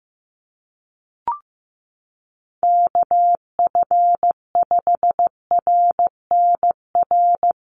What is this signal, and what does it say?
Morse code, heard as a keyed tone, about 15 wpm, 700 Hz. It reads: KF5RNR
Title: USA Call Signs 15wpm